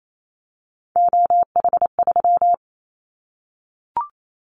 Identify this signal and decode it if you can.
Morse code, heard as a keyed tone, about 28 wpm, 700 Hz. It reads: OH3
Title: Callsign Prefixes for Top 25 DXCC - 28wpm